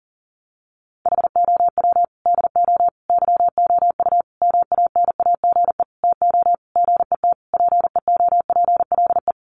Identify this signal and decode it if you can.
Morse code, heard as a keyed tone, about 40 wpm, 700 Hz. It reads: HOW DO YOU MANAGE TO GET PEOPLE
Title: Sentences from Top 1400 Words 40wpm